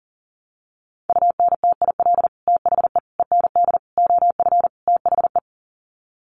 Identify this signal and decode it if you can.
Morse code, heard as a keyed tone, about 40 wpm, 700 Hz. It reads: UNTIL THE END OF THE
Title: Sets of 5 Words Encore 40wpm